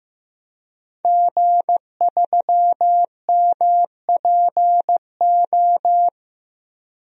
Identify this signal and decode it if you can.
Morse code, heard as a keyed tone, about 15 wpm, 700 Hz. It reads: G3MPO
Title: DXCC-100 Call Signs - 5 Characters - 15wpm